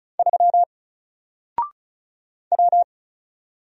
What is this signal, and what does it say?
Morse code, heard as a keyed tone, about 34 wpm, 700 Hz. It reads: 3 W
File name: Single-Letter-Number-35wpm